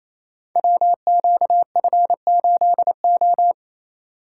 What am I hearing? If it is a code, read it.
Morse code, heard as a keyed tone, about 28 wpm, 700 Hz. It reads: WQF8O
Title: Code Groups - 5 Letter:Numbers - 28wpm